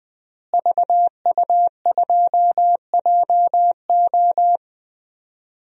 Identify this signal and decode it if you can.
Morse code, heard as a keyed tone, about 21 wpm, 700 Hz. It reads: VU2JO
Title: DXCC-100 Call Signs - 5 Characters - Repeated 3x - 20wpm